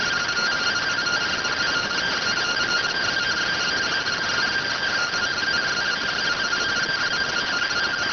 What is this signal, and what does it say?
Two-tone teleprinter signal, about 170 Hz shift.